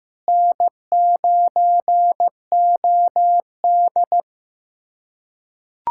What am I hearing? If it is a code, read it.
Morse code, heard as a keyed tone, about 15 wpm, 700 Hz. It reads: N9OD
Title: USA Callsign DE Callsign - 4 Characters - 15wpm